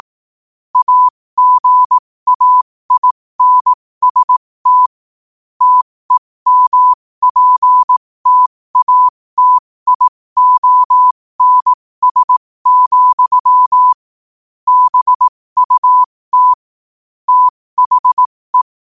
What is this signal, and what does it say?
Morse code, heard as a keyed tone, about 18 wpm, spaced as for 15 wpm, 1000 Hz. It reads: AGAINST TEMPTATIONS, BUT THE
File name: qotdfiles / 15wpm